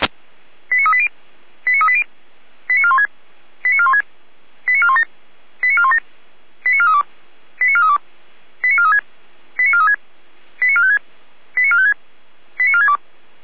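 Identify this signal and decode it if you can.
Five-tone selective call (ZVEI), heard as quick runs of five tones, tones of about 70 ms. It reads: 89380 89380 89416 89416 89417 89417 89432 89432 89436 89436 89456 89456 89462
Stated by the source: ZVEI five-tone selective call received on an SDR.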